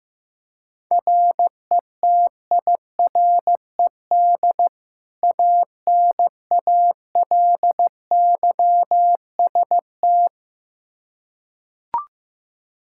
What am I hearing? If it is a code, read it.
Morse code, heard as a keyed tone, about 15 wpm, 700 Hz. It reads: RETIRED ANALYST
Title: K1USN SST QSO Practice 15wpm